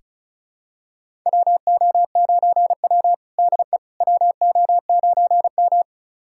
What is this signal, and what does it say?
Morse code, heard as a keyed tone, about 36 wpm, 700 Hz. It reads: WO9W DE WO9M
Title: USA Callsign DE Callsign - 4 Characters - 35wpm